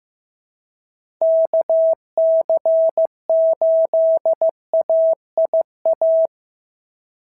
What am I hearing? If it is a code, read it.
Morse code, heard as a keyed tone, about 15 wpm, 690 Hz. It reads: KC8AIA
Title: SOTA QSO Practice 15wpm